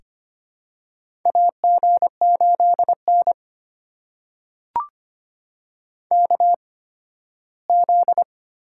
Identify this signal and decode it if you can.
Morse code, heard as a keyed tone, about 25 wpm, spaced as for 15 wpm, 700 Hz. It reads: AG8N KZ
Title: USA Call Signs - 4 Characters - 8x Character Spacing - 25wpm